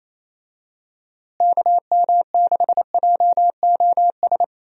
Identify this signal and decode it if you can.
Morse code, heard as a keyed tone, about 28 wpm, 700 Hz. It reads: KM6JOS
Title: USA Call Signs - 6 Characters - Repeated 3x - 28wpm